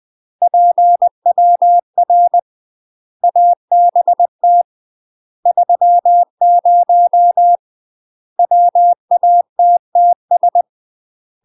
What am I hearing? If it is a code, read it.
Morse code, heard as a keyed tone, about 20 wpm, 700 Hz. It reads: PWR ABT 30 WATTS
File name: UK MASTER QSOs TRAINING / 20WPM / UK MASTER QSOs 0053